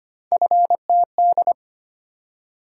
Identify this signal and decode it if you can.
Morse code, heard as a keyed tone, about 25 wpm, 700 Hz. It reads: FTD
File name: Three Letter-Number Combinations - 25wpm